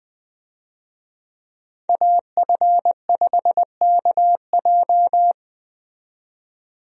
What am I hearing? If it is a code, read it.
Morse code, heard as a keyed tone, about 21 wpm, 700 Hz. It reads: AF5KJ
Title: USA Call Signs - 5 Characters - 20wpm